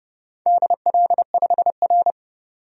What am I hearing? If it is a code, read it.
Morse code, heard as a keyed tone, about 29 wpm, 700 Hz. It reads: DL5R
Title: DXCC-100 Call Signs - 4 Characters - 30wpm